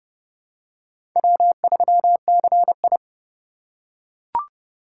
Morse code, heard as a keyed tone, about 26 wpm, 700 Hz. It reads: W3CI